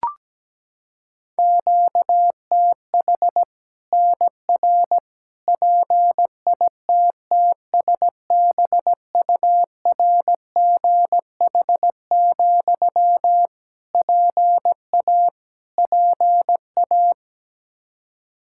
Morse code, heard as a keyed tone, about 17 wpm, 700 Hz. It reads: QTH NR PITTSBURGH, PA PA
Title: Practice QSO - Exchange 1 - Beginning - 17wpm